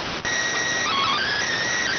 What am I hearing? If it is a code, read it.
SSTV (slow-scan television), PD 290.